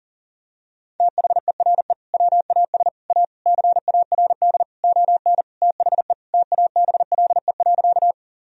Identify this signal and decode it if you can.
Morse code, heard as a keyed tone, about 40 wpm, 700 Hz. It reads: THERE WAS A CARD ON THE TABLE.
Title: Sentences from Top 1000 Words - Review Part 4 - 40wpm